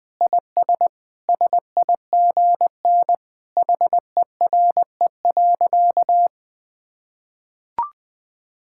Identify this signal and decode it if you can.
Morse code, heard as a keyed tone, about 21 wpm, 700 Hz. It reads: IS SIGN HERE.
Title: Sentences with 6-Word Phrases - 20wpm